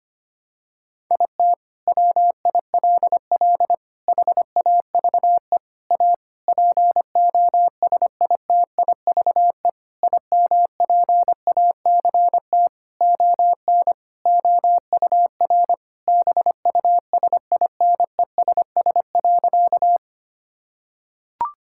Morse code, heard as a keyed tone, about 25 wpm, 700 Hz. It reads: IT WILL HAVE A POSITIVE IMPACT ON OUR BUSINESS.
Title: Sentences from Top 1000 Words - Review Part 6 - 25wpm